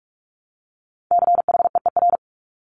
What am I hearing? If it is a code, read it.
Morse code, heard as a keyed tone, about 46 wpm, 700 Hz. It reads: CHEER
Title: Multi-letter Phonemes Repeated 3x 45wpm